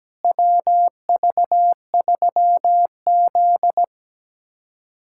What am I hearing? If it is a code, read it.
Morse code, heard as a keyed tone, about 17 wpm, 700 Hz. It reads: WV3Z
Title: USA Callsign DE Callsign - 4 Characters - 17wpm